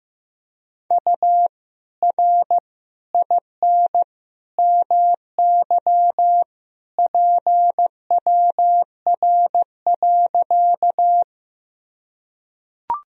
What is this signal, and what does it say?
Morse code, heard as a keyed tone, about 16 wpm, 700 Hz. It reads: U R IN MY PWR.
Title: Sentences from Top 500 Words and Abbreviations 15wpm